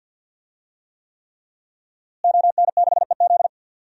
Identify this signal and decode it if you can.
Morse code, heard as a keyed tone, about 52 wpm, 700 Hz. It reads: ON6EZ
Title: DXCC-100 Call Signs - 5 Characters - Repeated 3x - 50wpm